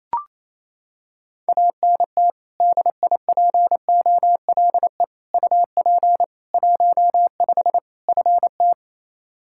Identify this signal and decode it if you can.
Morse code, heard as a keyed tone, about 28 wpm, 700 Hz. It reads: ANT DIPOLE UP 15 FT
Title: Practice QSO - Exchange 2 - Info - 28wpm